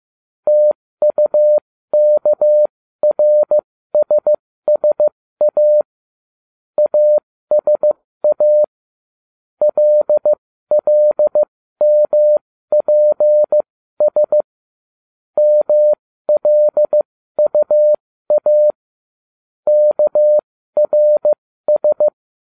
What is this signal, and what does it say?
Morse code, heard as a keyed tone, about 15 wpm, spaced as for 13 wpm, 600 Hz. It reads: TUKRSSA ASA LLMPS MLUA KRS